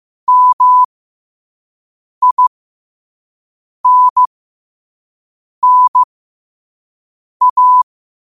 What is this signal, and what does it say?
Morse code, heard as a keyed tone, about 15 wpm, spaced as for 5 wpm, 1000 Hz. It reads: MINNA